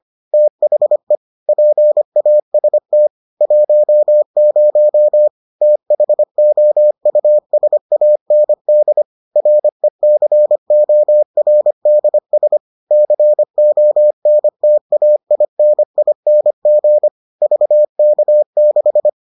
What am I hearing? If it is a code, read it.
Morse code, heard as a keyed tone, about 25 wpm, 600 Hz. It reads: THE PAST 10 THOUSAND RECORDS CONTAINING VK6